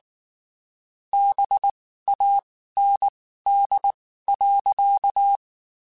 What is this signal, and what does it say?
Morse code, heard as a keyed tone, about 19 wpm, spaced as for 14 wpm, 800 Hz. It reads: BAND.
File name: (spaced) / QSOs Spaced-mp3s / kc7haa @18 sp